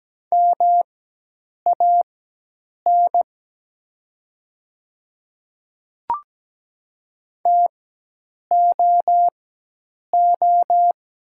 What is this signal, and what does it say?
Morse code, heard as a keyed tone, about 17 wpm, spaced as for 8 wpm, 700 Hz. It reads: MAN TOO
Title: 4x Character Spacing - Top 200 Words - 17wpm